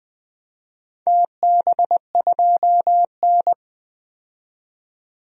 Morse code, heard as a keyed tone, about 20 wpm, 700 Hz. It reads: TB2N
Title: DXCC-100 Call Signs - 4 Characters - 20wpm